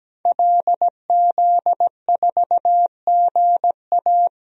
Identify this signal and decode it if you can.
Morse code, heard as a keyed tone, about 17 wpm, 700 Hz. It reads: LZ4GA
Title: DXCC-100 Call Signs - 5 Characters - 17wpm